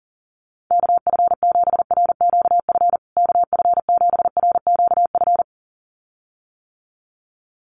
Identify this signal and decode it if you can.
Morse code, heard as a keyed tone, about 40 wpm, 700 Hz. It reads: KF7RQF KF7RQF
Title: USA Call Signs Repeated 2x 40wpm